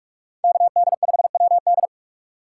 Morse code, heard as a keyed tone, about 48 wpm, 700 Hz. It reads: KD5WD